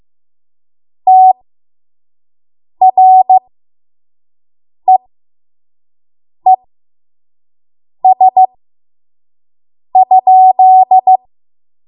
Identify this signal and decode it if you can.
Morse code, heard as a keyed tone, about 15 wpm, spaced as for 5 wpm, 750 Hz. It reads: TREES?
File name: Morse / 05WPM